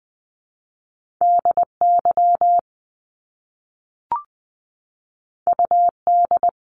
Morse code, heard as a keyed tone, about 20 wpm, 700 Hz. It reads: DY UD
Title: Top Two Letter Combinations in Words 20wpm